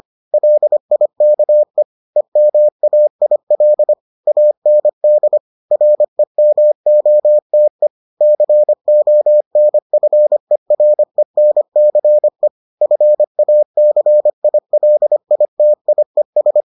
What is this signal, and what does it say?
Morse code, heard as a keyed tone, about 25 wpm, 600 Hz. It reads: LIKE EMAIL AND REMOTE CONFERENCE FACILITIES